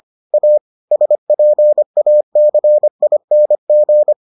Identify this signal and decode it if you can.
Morse code, heard as a keyed tone, about 25 wpm, 600 Hz. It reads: A SPACING